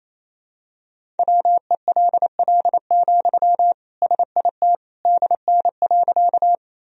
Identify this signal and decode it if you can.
Morse code, heard as a keyed tone, about 28 wpm, 700 Hz. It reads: WELL, SIT DN.
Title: Sentences from Top 500 Words and Abbreviations 28wpm